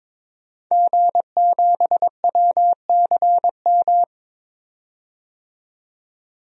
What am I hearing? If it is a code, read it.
Morse code, heard as a keyed tone, about 22 wpm, 700 Hz. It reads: G7WCM
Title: DXCC-100 Call Signs - 5 Characters - 22wpm